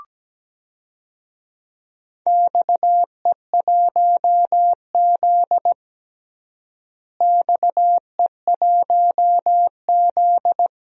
Morse code, heard as a keyed tone, about 17 wpm, 700 Hz. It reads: XE1Z XE1Z
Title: DXCC-100 Call Signs - 4 Characters - Repeated 2x - 17wpm